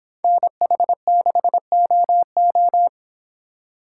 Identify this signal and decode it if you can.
Morse code, heard as a keyed tone, about 25 wpm, 700 Hz. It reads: NH6OO